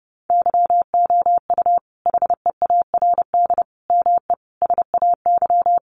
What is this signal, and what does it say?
Morse code, heard as a keyed tone, about 30 wpm, 700 Hz. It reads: YOU HEARD ME SAY
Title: Sentences from Top 500 Words 30wpm